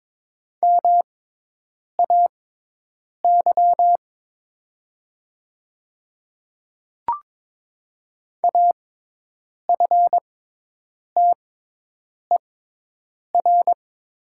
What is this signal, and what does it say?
Morse code, heard as a keyed tone, about 22 wpm, spaced as for 8 wpm, 700 Hz. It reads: MAY AFTER
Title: 6x Character Spacing - Top 200 Words - 22wpm